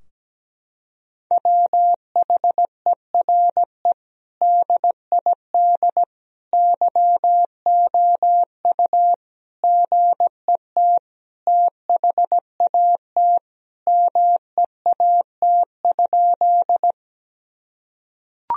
Morse code, heard as a keyed tone, about 17 wpm, 700 Hz. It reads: WHERE DID YOU GET THAT MEAT?
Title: Sentences from Top 1000 Words - Review Part 3 - 17wpm